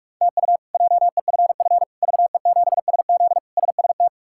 Morse code, heard as a keyed tone, about 46 wpm, 700 Hz. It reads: TU JEFF VE7SZ SST
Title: K1USN SST QSO Practice - 45wpm